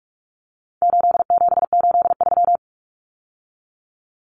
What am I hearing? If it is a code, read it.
Morse code, heard as a keyed tone, about 44 wpm, 700 Hz. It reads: 8783
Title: Numbers - 4-Digit 45wpm